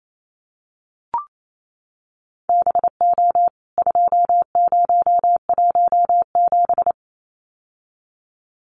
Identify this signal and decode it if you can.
Morse code, heard as a keyed tone, about 28 wpm, 700 Hz. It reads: BO 2017
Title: CWT Exchange 28wpm